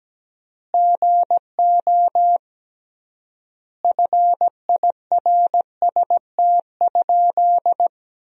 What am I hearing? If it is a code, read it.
Morse code, heard as a keyed tone, about 17 wpm, 700 Hz. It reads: GO FIRST?
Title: 3x Word-Spacing - Sentences from Top 100 Words - 17wpm